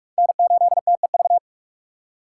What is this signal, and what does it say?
Morse code, heard as a keyed tone, about 44 wpm, 700 Hz. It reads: N8TEV
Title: USA Call Signs - 6x Character Spacing - 45wpm